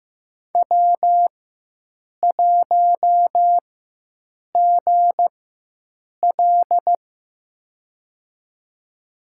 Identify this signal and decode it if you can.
Morse code, heard as a keyed tone, about 15 wpm, spaced as for 7 wpm, 700 Hz. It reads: W1GL
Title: USA Call Signs - 4 Characters - 4x Character Spacing - 15wpm